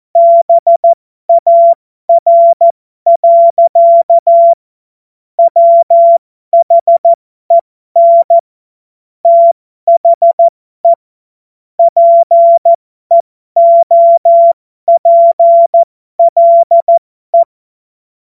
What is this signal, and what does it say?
Morse code, heard as a keyed tone, about 14 wpm, spaced as for 12 wpm, 680 Hz. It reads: BAR. WHEN THE PEOPLE
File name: Morsekurs / Witze